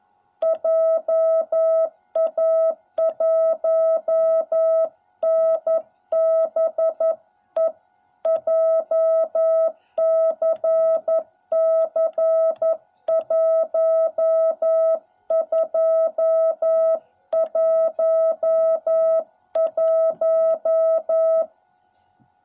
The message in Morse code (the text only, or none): JA1NBE JCC 1211